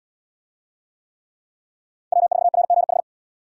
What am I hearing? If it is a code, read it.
Morse code, heard as a keyed tone, about 67 wpm, 700 Hz. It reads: W4RGH